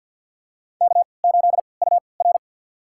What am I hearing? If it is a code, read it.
Morse code, heard as a keyed tone, about 50 wpm, spaced as for 28 wpm, 700 Hz. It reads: K8UR
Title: USA Call Signs - 3x Character Spacing - 50wpm